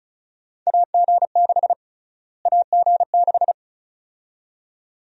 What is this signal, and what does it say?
Morse code, heard as a keyed tone, about 34 wpm, 700 Hz. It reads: AG6 AG6
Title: USA Call Sign Prefixes - Repeated 2x - 35wpm